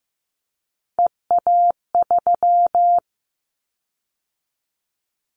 Morse code, heard as a keyed tone, about 15 wpm, 700 Hz. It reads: EA3
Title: Callsign Prefixes for Top 25 DXCC - 15wpm